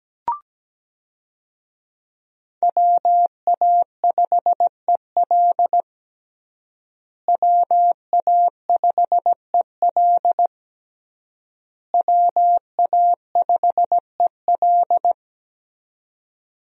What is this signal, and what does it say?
Morse code, heard as a keyed tone, about 17 wpm, 700 Hz. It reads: WA5EL WA5EL WA5EL
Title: USA Call Signs - 5 Characters - Repeated 3x - 17wpm